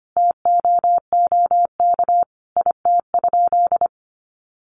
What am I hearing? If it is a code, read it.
Morse code, heard as a keyed tone, about 25 wpm, 700 Hz. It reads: TOOK IT?
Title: Sentences from Top 300 Words 25wpm